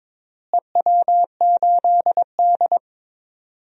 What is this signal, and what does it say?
Morse code, heard as a keyed tone, about 22 wpm, 700 Hz. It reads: EW8D
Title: DXCC-100 Call Signs - 4 Characters - Repeated 2x - 22wpm